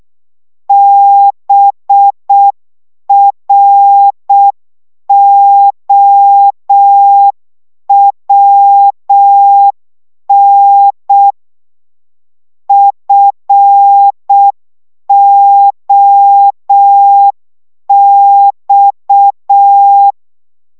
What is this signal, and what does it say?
Morse code, heard as a keyed tone, about 6 wpm, 800 Hz. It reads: BROWN FOX
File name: szamolo / CW 5 baud